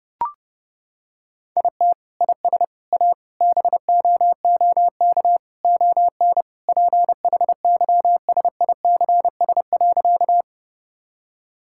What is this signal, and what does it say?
Morse code, heard as a keyed tone, about 32 wpm, 700 Hz. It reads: IT IS A BOOK ON PHYSICS.
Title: Sentences from Top 1000 Words - Review Part 5 - 30wpm